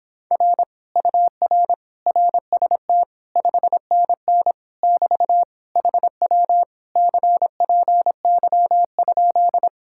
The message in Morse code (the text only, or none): R UR RST 5NN <BT> HW CPY?